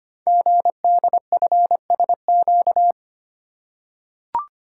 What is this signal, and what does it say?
Morse code, heard as a keyed tone, about 25 wpm, 700 Hz. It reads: GDFSQ
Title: Code Groups - 5 Letters - 25wpm